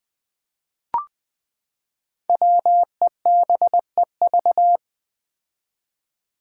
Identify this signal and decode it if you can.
Morse code, heard as a keyed tone, about 20 wpm, 700 Hz. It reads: WEBEV
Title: Five Challenging Letter-Number Combinations 20wpm